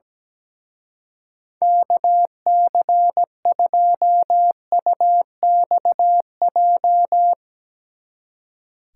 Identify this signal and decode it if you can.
Morse code, heard as a keyed tone, about 17 wpm, 700 Hz. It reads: KC2UXJ
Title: Speed Racing - USA Call Signs 17wpm